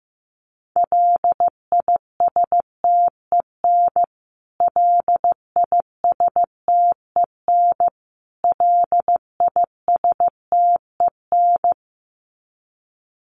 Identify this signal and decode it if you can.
Morse code, heard as a keyed tone, about 15 wpm, 700 Hz. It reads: LISTEN LISTEN LISTEN
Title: Top 500 Words Repeated 15wpm